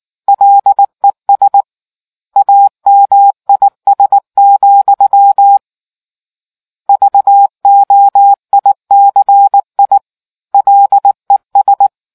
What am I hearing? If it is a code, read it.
Morse code, heard as a keyed tone, about 19 wpm, 800 Hz. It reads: LES AMIS, VOICI LES